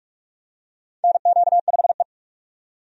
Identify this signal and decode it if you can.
Morse code, heard as a keyed tone, about 42 wpm, 700 Hz. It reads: NQ5E
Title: USA Call Signs - 4 Characters - 45wpm